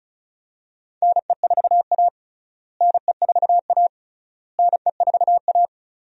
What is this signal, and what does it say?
Morse code, heard as a keyed tone, about 34 wpm, 700 Hz. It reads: NE4A NE4A NE4A
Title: USA Call Signs - 4 Characters - Repeated 3x - 35wpm